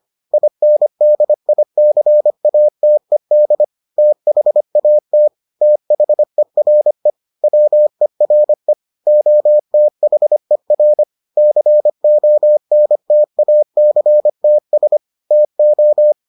Morse code, heard as a keyed tone, about 25 wpm, 600 Hz. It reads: INDICATED THAT THERE WERE OTHER CONTACTS TO